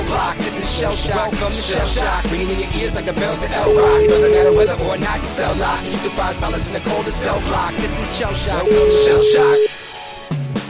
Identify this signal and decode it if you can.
Ringback tone (Europe, 425 Hz), with speech.